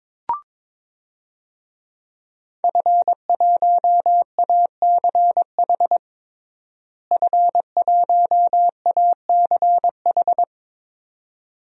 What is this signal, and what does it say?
Morse code, heard as a keyed tone, about 22 wpm, 700 Hz. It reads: F1ACH F1ACH
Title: DXCC-100 Call Signs - 5 Characters - Repeated 2x - 22wpm